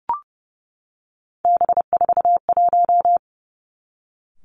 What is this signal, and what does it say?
Morse code, heard as a keyed tone, about 30 wpm, 700 Hz. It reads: B41
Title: Three Challenging Letter-Number Combinations 30wpm